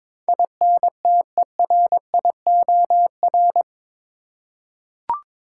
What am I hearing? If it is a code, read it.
Morse code, heard as a keyed tone, about 22 wpm, 700 Hz. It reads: INTERIOR
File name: practice / 22wpm / Eight Letter Words - 22wpm